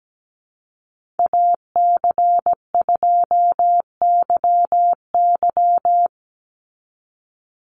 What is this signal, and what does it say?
Morse code, heard as a keyed tone, about 17 wpm, 700 Hz. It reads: AC2YY
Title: USA Call Signs 17wpm